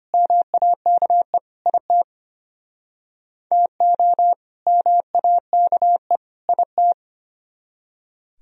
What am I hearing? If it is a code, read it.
Morse code, heard as a keyed tone, about 34 wpm, 700 Hz. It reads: MAKE IT TO MAKE IT